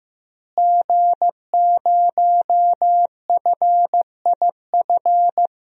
Morse code, heard as a keyed tone, about 15 wpm, 700 Hz. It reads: G0FIF
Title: DXCC-100 Call Signs - 5 Characters - Repeated 3x - 15wpm